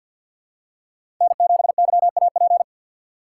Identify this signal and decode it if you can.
Morse code, heard as a keyed tone, about 50 wpm, 700 Hz. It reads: N7YAP
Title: USA Call Signs - 5 Characters - 2x Character Spacing - 50wpm